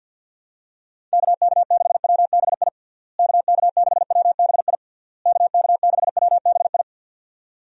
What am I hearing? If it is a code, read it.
Morse code, heard as a keyed tone, about 49 wpm, 700 Hz. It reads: KK6WBI KK6WBI KK6WBI
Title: USA Call Signs - 6 Characters - Repeated 3x - 50wpm